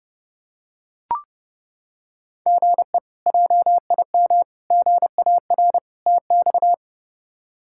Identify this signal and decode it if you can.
Morse code, heard as a keyed tone, about 30 wpm, 700 Hz. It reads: GE JIM GAR TX
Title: K1USN SST QSO Practice - 30wpm